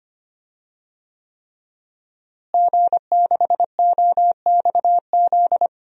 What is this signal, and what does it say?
Morse code, heard as a keyed tone, about 25 wpm, 700 Hz. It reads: G6OXZ
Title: DXCC-100 Call Signs - 5 Characters - Repeated 3x - 25wpm